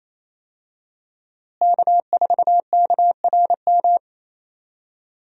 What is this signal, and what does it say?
Morse code, heard as a keyed tone, about 28 wpm, 700 Hz. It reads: K4KRM
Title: USA Call Signs - 5 Characters - Repeated 3x - 28wpm